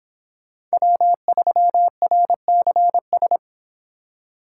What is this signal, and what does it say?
Morse code, heard as a keyed tone, about 31 wpm, 700 Hz. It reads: W3RCS